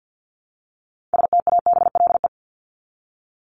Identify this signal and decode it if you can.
Morse code, heard as a keyed tone, about 50 wpm, 700 Hz. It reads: STABLE
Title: Top 1251-1500 Words 50wpm